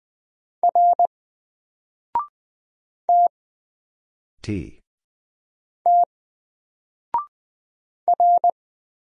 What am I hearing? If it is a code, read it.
Morse code, heard as a keyed tone, about 20 wpm, 700 Hz. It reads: R T T R
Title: Single Letter-Number Alt 20wpm